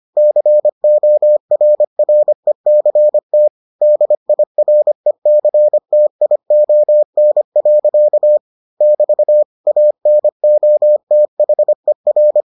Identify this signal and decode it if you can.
Morse code, heard as a keyed tone, about 25 wpm, 600 Hz. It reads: CORRECT DIRECTION. <BT> ANOTHER